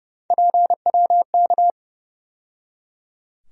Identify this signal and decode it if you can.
Morse code, heard as a keyed tone, about 30 wpm, 700 Hz. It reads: PWK
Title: USA Call Sign Postfixes - 3 Letters 30wpm